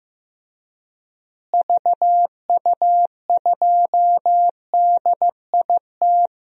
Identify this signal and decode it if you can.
Morse code, heard as a keyed tone, about 15 wpm, 700 Hz. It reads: VU2DIT
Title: DXCC-100 Call Signs - 6 Characters - Repeated 3x - 15wpm